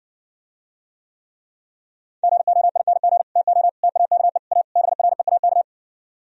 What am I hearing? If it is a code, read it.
Morse code, heard as a keyed tone, about 59 wpm, 700 Hz. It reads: GOING TO TAKE A BREAK